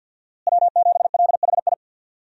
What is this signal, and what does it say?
Morse code, heard as a keyed tone, about 50 wpm, 700 Hz. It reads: W7LHI
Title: USA Call Signs - 5 Characters - Repeated 3x - 50wpm